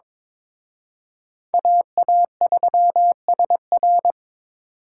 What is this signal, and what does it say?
Morse code, heard as a keyed tone, about 26 wpm, 700 Hz. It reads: AA3SR
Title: Speed Racing - USA 2x2 Call Signs 22wpm